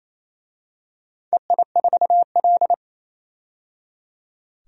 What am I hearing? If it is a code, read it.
Morse code, heard as a keyed tone, about 28 wpm, 700 Hz. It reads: EI4L
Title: DXCC-100 Call Signs - 4 Characters - 28wpm